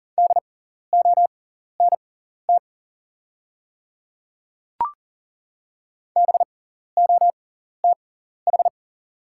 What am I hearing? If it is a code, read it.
Morse code, heard as a keyed tone, about 40 wpm, spaced as for 14 wpm, 690 Hz. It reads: DONT BOTH
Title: 6x Character Spacing - Top 300 Words - 40wpm